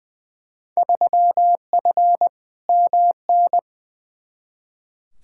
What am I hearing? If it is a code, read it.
Morse code, heard as a keyed tone, about 20 wpm, 700 Hz. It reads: 3F MN
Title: Field Day Exchanges Encore - 20wpm